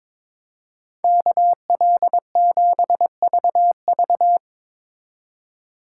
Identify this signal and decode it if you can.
Morse code, heard as a keyed tone, about 22 wpm, 700 Hz. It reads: KL7VV
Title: USA Call Signs - 5 Characters - 4x Character Spacing - 22wpm